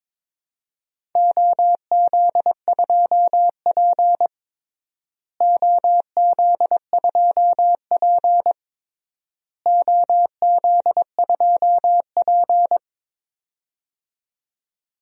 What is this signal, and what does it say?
Morse code, heard as a keyed tone, about 22 wpm, 700 Hz. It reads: OZ2P OZ2P OZ2P
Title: DXCC-100 Call Signs - 4 Characters - Repeated 3x - 22wpm